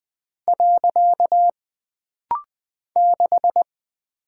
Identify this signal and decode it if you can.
Morse code, heard as a keyed tone, about 20 wpm, 700 Hz. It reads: . 6
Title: Single Character (Letter:Number:Punctuation:Prosign) - Rapid-Fire - 20wpm